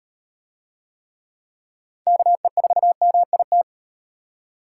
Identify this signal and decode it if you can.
Morse code, heard as a keyed tone, about 36 wpm, 700 Hz. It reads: KE4MIT